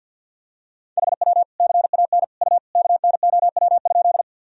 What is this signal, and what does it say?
Morse code, heard as a keyed tone, about 46 wpm, 700 Hz. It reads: HW CAN U KNOW?
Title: Sentences from Top 100 Words and Abbreviations 50wpm